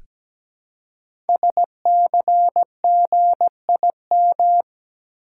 Morse code, heard as a keyed tone, about 17 wpm, 700 Hz. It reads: SCGIM
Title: Code Groups - 5 Letter:Numbers - 17wpm